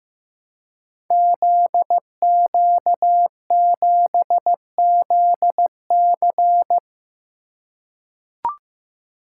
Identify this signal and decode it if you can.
Morse code, heard as a keyed tone, about 15 wpm, 700 Hz. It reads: ZQ7ZC
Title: Code Groups - 5 Letter:Numbers - 15wpm